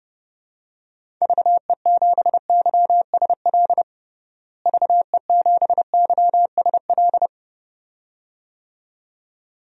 Morse code, heard as a keyed tone, about 29 wpm, 700 Hz. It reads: VE7YSL VE7YSL
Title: DXCC-100 Call Signs - 6 Characters - Repeated 2x - 30wpm